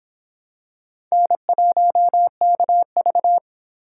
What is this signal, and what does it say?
Morse code, heard as a keyed tone, about 26 wpm, 700 Hz. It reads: N1KV